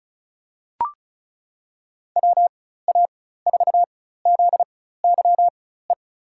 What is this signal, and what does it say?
Morse code, heard as a keyed tone, about 34 wpm, spaced as for 16 wpm, 700 Hz. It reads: WA4ZYE
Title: USA Call Signs - 6 Characters - 4x Character Spacing - 35wpm